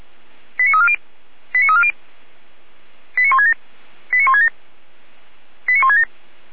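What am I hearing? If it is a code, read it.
Five-tone selective call (ZVEI), heard as quick runs of five tones, tones of about 70 ms. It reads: 89340 89340 89167 89167 89167
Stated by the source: ZVEI five-tone selective call received by SDR.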